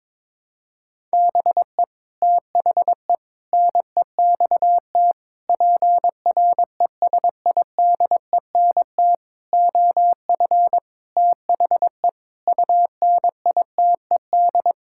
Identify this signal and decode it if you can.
Morse code, heard as a keyed tone, about 22 wpm, 700 Hz. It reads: BE THE NEXT PRESIDENT OF THE UNITED
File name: practice / 22wpm / Sets of 8 Words - 22wpm